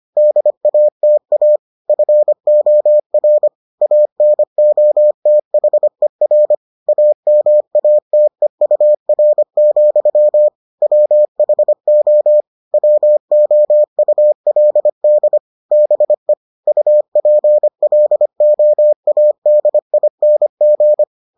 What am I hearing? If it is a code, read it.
Morse code, heard as a keyed tone, about 25 wpm, 600 Hz. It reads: DATA FOR ANOTHER AMATEUR, WHO WOULD BE UPLOADING